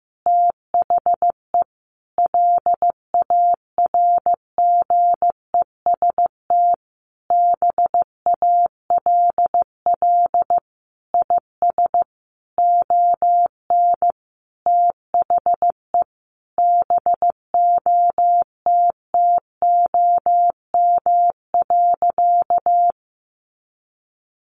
Morse code, heard as a keyed tone, about 15 wpm, 700 Hz. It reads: THE LARGEST BALL IS ON THE BOTTOM.
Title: Easy Phrases 15wpm